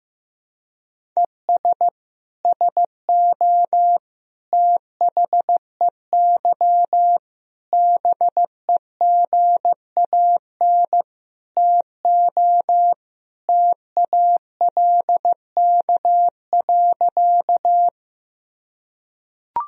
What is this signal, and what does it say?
Morse code, heard as a keyed tone, about 15 wpm, 700 Hz. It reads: ES SO THEY BEGAN TO TALK.
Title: Sentences from Top 400 Words and Abbreviations 15wpm